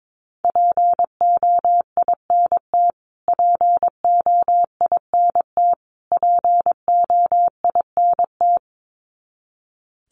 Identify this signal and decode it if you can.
Morse code, heard as a keyed tone, about 22 wpm, 700 Hz. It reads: POINT POINT POINT
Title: Top 200 Words Repeated 22wpm